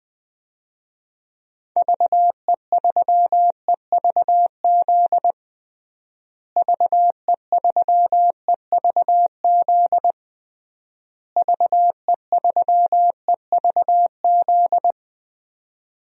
Morse code, heard as a keyed tone, about 20 wpm, 700 Hz. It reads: VE3EVZ VE3EVZ VE3EVZ
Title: DXCC-100 Call Signs - 6 Characters - Repeated 3x - 20wpm